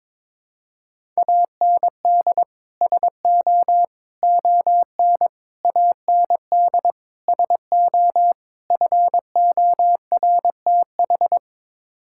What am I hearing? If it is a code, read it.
Morse code, heard as a keyed tone, about 22 wpm, 700 Hz. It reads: AND SO ON AND SO FORTH
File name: practice / 22wpm / Sets of 6 Words - 22wpm